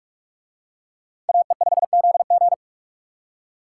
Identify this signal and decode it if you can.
Morse code, heard as a keyed tone, about 39 wpm, 700 Hz. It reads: AE5ZG